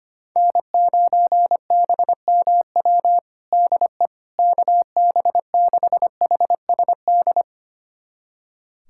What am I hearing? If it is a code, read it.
Morse code, heard as a keyed tone, about 25 wpm, 700 Hz. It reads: N9BMW DE KB6HSD
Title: USA Callsign DE Callsign 25wpm